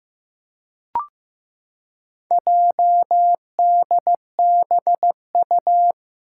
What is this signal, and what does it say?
Morse code, heard as a keyed tone, about 15 wpm, 700 Hz. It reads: JDBU
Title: Four Challenging Letter-Number Combinations 15wpm